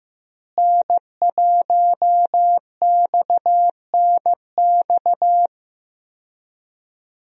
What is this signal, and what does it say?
Morse code, heard as a keyed tone, about 15 wpm, 700 Hz. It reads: N1XNX
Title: USA Call Signs - 5 Characters - Repeated 3x - 15wpm